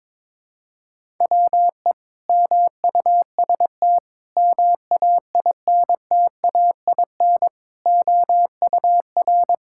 Morse code, heard as a keyed tone, about 22 wpm, 700 Hz. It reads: WE MUST MAINTAIN OUR
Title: Sentences from Top 1000 Words - Review Part 5 - 22wpm